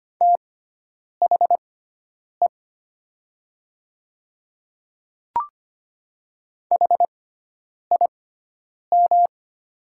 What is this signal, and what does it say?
Morse code, heard as a keyed tone, about 25 wpm, spaced as for 9 wpm, 700 Hz. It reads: THE HIM
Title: 6x Character Spacing - Top 100 Words - 25wpm